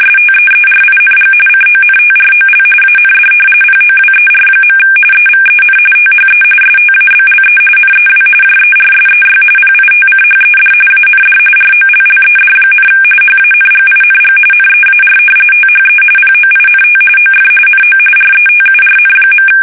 RTTY, about 45.45 baud. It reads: STEERING CONTROL AND POWER. EVEN THE STABLE NEW JERSEY PITCHED AND ROLLED, HANGING NEAR THE POINT OF NO RETURN.
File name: fsk850 / battleshipnj